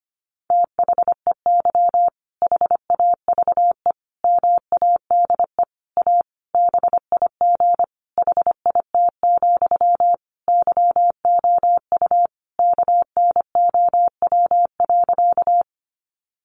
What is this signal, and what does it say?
Morse code, heard as a keyed tone, about 25 wpm, 700 Hz. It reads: THEY HAVE MADE A BIG HIT, YOU KNOW.
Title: Sentences from Top 1200 Words 25wpm